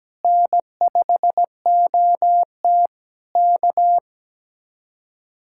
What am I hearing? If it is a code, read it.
Morse code, heard as a keyed tone, about 17 wpm, 700 Hz. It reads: N5OT K
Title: Speed-Racing Common QSO Phrases Encore - 17wpm